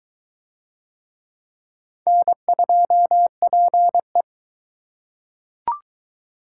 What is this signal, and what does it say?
Morse code, heard as a keyed tone, about 22 wpm, 700 Hz. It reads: N2PE